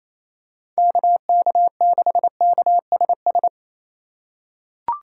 Morse code, heard as a keyed tone, about 28 wpm, 700 Hz. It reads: KK6KSS
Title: USA Call Signs - 6 Characters - 3x Character Spacing - 28wpm